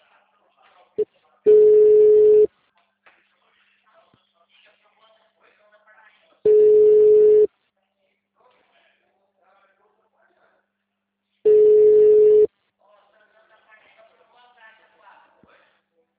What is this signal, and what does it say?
Ringback tone (Europe, 425 Hz), with speech.
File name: out- -2016- - - .10685